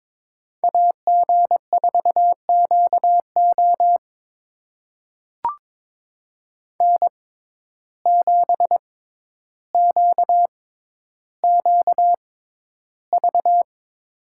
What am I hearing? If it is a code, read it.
Morse code, heard as a keyed tone, about 22 wpm, spaced as for 11 wpm, 700 Hz. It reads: AG4QO N7QQV
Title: USA Call Signs - 5 Characters - 6x Character Spacing - 22wpm